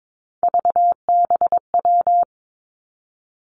Morse code, heard as a keyed tone, about 22 wpm, 700 Hz. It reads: VBW
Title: Three Challenging Letter-Number Combinations 22wpm